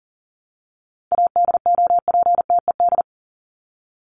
Morse code, heard as a keyed tone, about 40 wpm, 700 Hz. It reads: ADOPTED
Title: ED Suffix - Top 100 Words with Base Words 40wpm